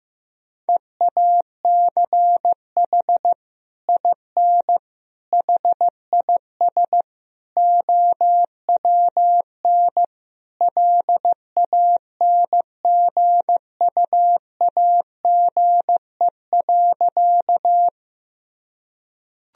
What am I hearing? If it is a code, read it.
Morse code, heard as a keyed tone, about 16 wpm, 700 Hz. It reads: EACH IN HIS OWN LANGUAGE.
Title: Sentences from Top 1000 Words - Review Part 3 - 15wpm